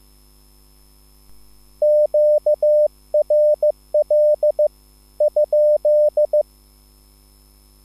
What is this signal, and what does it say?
Morse code, heard as a keyed tone, about 15 wpm, spaced as for 13 wpm, 600 Hz. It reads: QRL?
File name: QSL / CW lesson